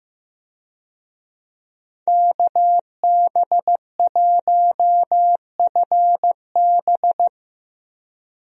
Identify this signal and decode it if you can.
Morse code, heard as a keyed tone, about 15 wpm, 700 Hz. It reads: KB1FB
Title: USA Call Signs - 5 Characters - 15wpm